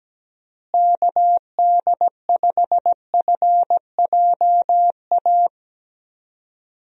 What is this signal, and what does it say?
Morse code, heard as a keyed tone, about 17 wpm, 700 Hz. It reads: KD5FJA
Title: USA Call Signs - 6 Characters - Repeated 2x - 17wpm